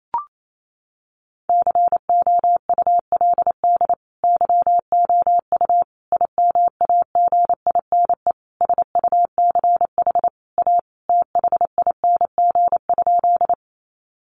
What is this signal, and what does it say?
Morse code, heard as a keyed tone, about 28 wpm, 700 Hz. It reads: COULD YOU IMAGINE SUCH A THING?
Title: Sentences from Top 1400 Words 28wpm